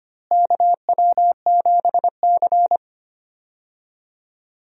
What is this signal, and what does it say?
Morse code, heard as a keyed tone, about 25 wpm, 700 Hz. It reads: KW7C
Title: USA Call Signs - 4 Characters - Repeated 2x - 25wpm